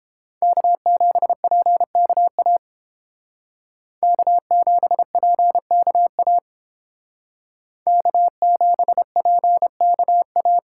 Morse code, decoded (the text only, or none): K7PKA K7PKA K7PKA